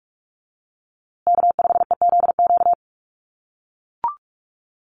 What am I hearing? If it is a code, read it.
Morse code, heard as a keyed tone, about 44 wpm, 690 Hz. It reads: K5EZQ
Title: USA Call Signs Repeated 2x 45wpm